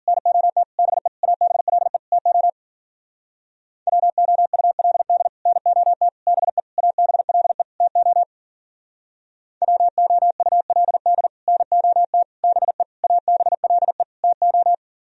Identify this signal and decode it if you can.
Morse code, heard as a keyed tone, about 51 wpm, 700 Hz. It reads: NOT BE ABLE TO WOULD NOT BE ABLE TO WOULD NOT BE ABLE TO